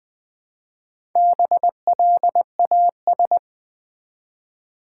Morse code, heard as a keyed tone, about 20 wpm, 700 Hz. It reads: BLAS
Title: Build A Word - 1 Letter At A Time - 5-Letter Words - 20wpm